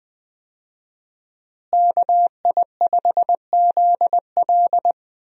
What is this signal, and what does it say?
Morse code, heard as a keyed tone, about 20 wpm, 700 Hz. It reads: KI5ZL
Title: USA Call Signs - 5 Characters - Repeated 2x - 20wpm